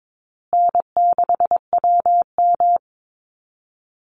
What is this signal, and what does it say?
Morse code, heard as a keyed tone, about 22 wpm, 700 Hz. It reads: N6WM
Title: Call Signs Repeated 2x - 22wpm